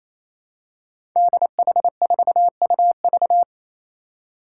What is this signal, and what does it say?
Morse code, heard as a keyed tone, about 28 wpm, 700 Hz. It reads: DH4UV
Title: Five Challenging Letter-Number Combinations 28wpm